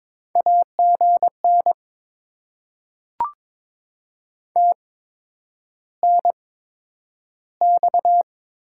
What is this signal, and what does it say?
Morse code, heard as a keyed tone, about 22 wpm, spaced as for 9 wpm, 700 Hz. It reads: AGN TNX
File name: Top 20 QSO Elements - 8x Character Spacing - 22wpm